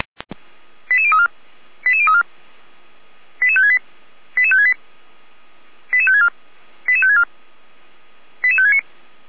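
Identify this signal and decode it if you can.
Five-tone selective call (ZVEI), heard as quick runs of five tones, tones of about 70 ms. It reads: 88034 88034 88568 88568 88564 88564 88569
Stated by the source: Five-tone ZVEI selective call received by SDR.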